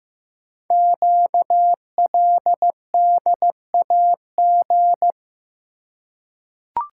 Morse code, heard as a keyed tone, about 15 wpm, 700 Hz. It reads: QLDAG
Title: Code Groups - 5 Letters - 15wpm